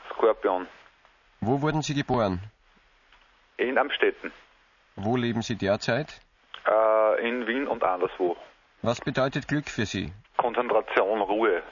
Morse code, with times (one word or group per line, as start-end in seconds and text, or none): none